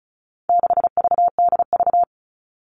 Morse code, heard as a keyed tone, about 34 wpm, 700 Hz. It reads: 6VDV
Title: All Cut Numbers - 4-Digit 35wpm